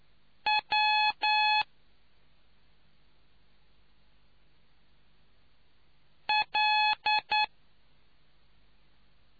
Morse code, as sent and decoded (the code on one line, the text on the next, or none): .-- / .-..
W L